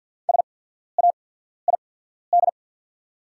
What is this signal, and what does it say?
Morse code, heard as a keyed tone, about 50 wpm, spaced as for 14 wpm, 690 Hz. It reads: SAID
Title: 8x Character Spacing - Top 100 Words - 50wpm